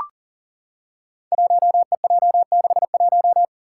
Morse code, heard as a keyed tone, about 40 wpm, 700 Hz. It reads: 1EJ61
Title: Five Challenging Letter-Number Combinations 40wpm